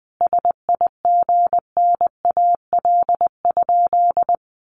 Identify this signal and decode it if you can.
Morse code, heard as a keyed tone, about 20 wpm, 700 Hz. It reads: SIGNAL?
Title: Sentences from Top 1400 Words 20wpm